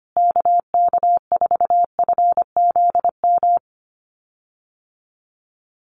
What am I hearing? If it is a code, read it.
Morse code, heard as a keyed tone, about 25 wpm, 700 Hz. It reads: KK4FZM
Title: USA Call Signs Repeated 2x 25wpm